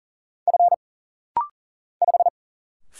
Morse code, heard as a keyed tone, about 40 wpm, 700 Hz. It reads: F 5
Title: Single Letter-Number - Rapid-Fire 40wpm